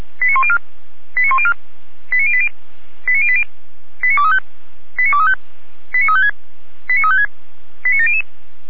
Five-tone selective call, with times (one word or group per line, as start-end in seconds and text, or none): 0.2-0.6 89104
1.2-1.5 89104
2.1-2.5 89080
3.1-3.4 89080
4.0-4.4 89235
5.0-5.3 89235
5.9-6.3 89356
6.9-7.3 89356
7.8-8.2 89700